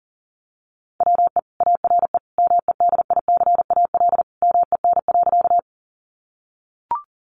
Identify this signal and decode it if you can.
Morse code, heard as a keyed tone, about 40 wpm, 700 Hz. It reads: WE ARE MEDICAL MEN.
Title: Sentences from Top 800 Words 40wpm